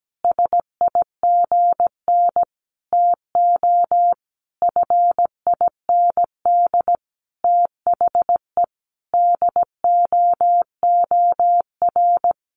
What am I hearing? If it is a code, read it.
Morse code, heard as a keyed tone, about 17 wpm, 700 Hz. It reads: SIGN TO FIND THE DOOR
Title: Sentences from Top 1000 Words 17wpm